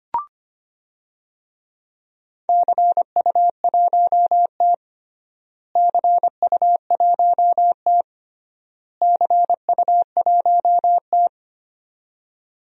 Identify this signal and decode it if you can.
Morse code, heard as a keyed tone, about 25 wpm, 700 Hz. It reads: CU1T CU1T CU1T
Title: DXCC-100 Call Signs - 4 Characters - Repeated 3x - 25wpm